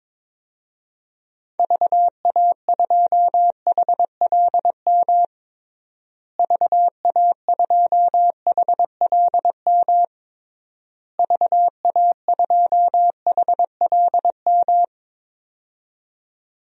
Morse code, heard as a keyed tone, about 22 wpm, 700 Hz. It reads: VA2HLM VA2HLM VA2HLM
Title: DXCC-100 Call Signs - 6 Characters - Repeated 3x - 22wpm